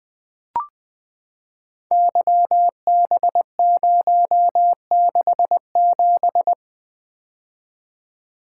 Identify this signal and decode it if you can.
Morse code, heard as a keyed tone, about 20 wpm, 700 Hz. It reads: YB067
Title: Code Groups - 5 Letter:Numbers - 20wpm